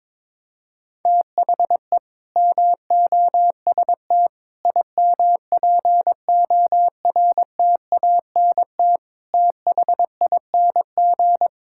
Morse code, heard as a keyed tone, about 22 wpm, 700 Hz. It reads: THE MOST IMPORTANT THING
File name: practice / 22wpm / Sentences with 4-Word Phrases - 22wpm